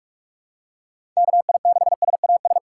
Morse code, heard as a keyed tone, about 44 wpm, 700 Hz. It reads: KI6SAS
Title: USA Call Signs - 2x Character Spacing - 45wpm